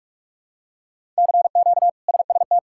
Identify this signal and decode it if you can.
Morse code, heard as a keyed tone, about 46 wpm, 700 Hz. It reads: CQ SST
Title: K1USN SST QSO Practice - 45wpm